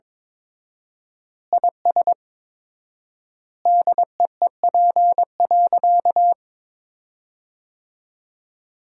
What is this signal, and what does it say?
Morse code, heard as a keyed tone, about 22 wpm, 700 Hz. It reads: IS DEEP.
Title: 4x Word-Spacing - Sentences from Top 500 Words - 22wpm